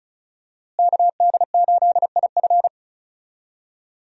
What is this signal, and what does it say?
Morse code, heard as a keyed tone, about 36 wpm, 700 Hz. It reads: KD8IF
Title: USA Call Signs - 5 Characters - Repeated 2x - 35wpm